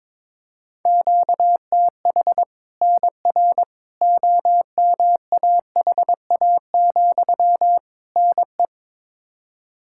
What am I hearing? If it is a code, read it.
Morse code, heard as a keyed tone, about 21 wpm, 700 Hz. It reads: QTH NR OMAHA, NE
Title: Practice QSO - Exchange 1 - Beginning - 22wpm